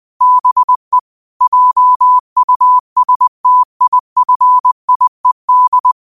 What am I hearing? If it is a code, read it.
Morse code, heard as a keyed tone, about 20 wpm, 1000 Hz. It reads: BE JUSTIFIED
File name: qotdfiles / 20wpm